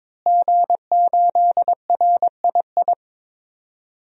Morse code, heard as a keyed tone, about 22 wpm, 700 Hz. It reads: G8RII
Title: DXCC-100 Call Signs - 5 Characters - Repeated 3x - 22wpm